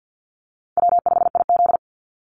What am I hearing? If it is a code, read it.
Morse code, heard as a keyed tone, about 49 wpm, 700 Hz. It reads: W5IZ
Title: Call Signs Repeated 3x - 50wpm